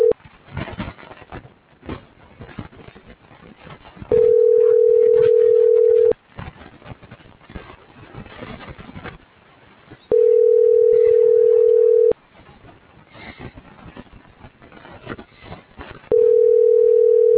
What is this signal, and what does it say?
Ringback tone (North America), 440 + 480 Hz.